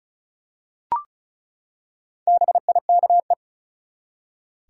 Morse code, heard as a keyed tone, about 36 wpm, 700 Hz. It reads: BIKE
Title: Four Letter Words - 35wpm